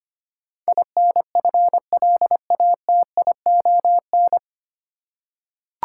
Morse code, heard as a keyed tone, about 25 wpm, 700 Hz. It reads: INFLATION
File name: practice / 25wpm / 9 Letter Words - First 3 Letters Given - 25wpm